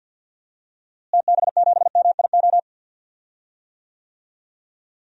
Morse code, heard as a keyed tone, about 52 wpm, 700 Hz. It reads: TB7MIO
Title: DXCC-100 Call Signs - 6 Characters - Repeated 2x - 50wpm